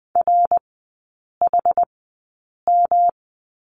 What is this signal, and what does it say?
Morse code, heard as a keyed tone, about 20 wpm, spaced as for 8 wpm, 700 Hz. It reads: RHM
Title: 114 Lesson 19e - Letter w - Three Characters 20wpm